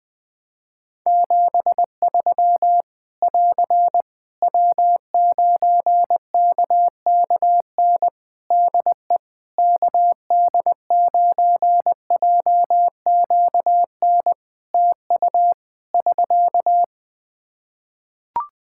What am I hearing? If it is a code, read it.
Morse code, heard as a keyed tone, about 20 wpm, 700 Hz. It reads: 73 <AR> W9KKN DE KD9JQN TU <SK>
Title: Practice QSO - Exchange 4 - Ending - 20wpm